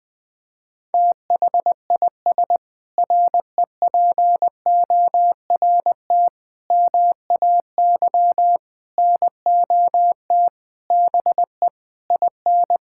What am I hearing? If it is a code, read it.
Morse code, heard as a keyed tone, about 21 wpm, 700 Hz. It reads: THIS REPORT MAY NOT BE IN
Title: Sentences with 7-Word Phrases - Spoken Beginning - 20wpm